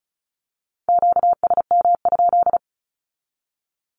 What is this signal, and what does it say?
Morse code, heard as a keyed tone, about 36 wpm, 700 Hz. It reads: QSM?
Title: Q-Codes with Spoken Meaning - 35wpm